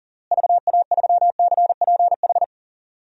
Morse code, heard as a keyed tone, about 37 wpm, 700 Hz. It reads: VA3CPH
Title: DXCC-100 Call Signs - 6 Characters - Repeated 3x - 40wpm